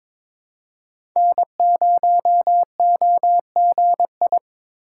Morse code, heard as a keyed tone, about 22 wpm, 700 Hz. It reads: N0OGI
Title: USA Call Signs - 5 Characters - 2x Character Spacing - 22wpm